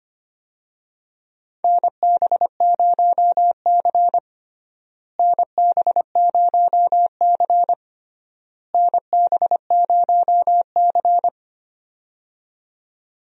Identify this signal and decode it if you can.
Morse code, heard as a keyed tone, about 25 wpm, 700 Hz. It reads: NB0C NB0C NB0C
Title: USA Call Signs - 4 Characters - Repeated 3x - 25wpm